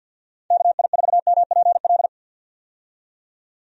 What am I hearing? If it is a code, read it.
Morse code, heard as a keyed tone, about 46 wpm, 700 Hz. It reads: KI4MPL
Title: USA Call Signs - 6 Characters - 50wpm